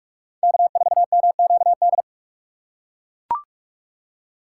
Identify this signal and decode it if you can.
Morse code, heard as a keyed tone, about 41 wpm, 700 Hz. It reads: K4MQD